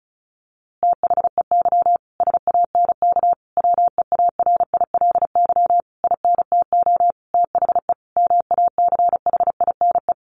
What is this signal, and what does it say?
Morse code, heard as a keyed tone, about 34 wpm, 700 Hz. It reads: THEY SANK WEARILY INTO THE MACHINE
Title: A Sound Of Thunder 35wpm